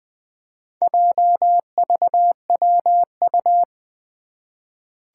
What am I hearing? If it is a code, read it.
Morse code, heard as a keyed tone, about 20 wpm, 700 Hz. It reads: JVWU
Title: Four Challenging Letter-Number Combinations 20wpm